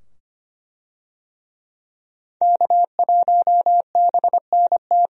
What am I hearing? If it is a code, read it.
Morse code, heard as a keyed tone, about 25 wpm, 700 Hz. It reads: K1BNT
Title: USA Call Signs - 5 Characters - 25wpm